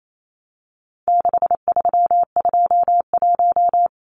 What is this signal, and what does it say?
Morse code, heard as a keyed tone, about 28 wpm, 700 Hz. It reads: 6321
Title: Numbers - 4-Digit 28wpm